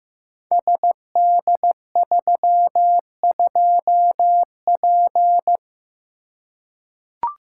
Morse code, heard as a keyed tone, about 15 wpm, 700 Hz. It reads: SD32P
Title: Code Groups - 5 Letter:Numbers - 15wpm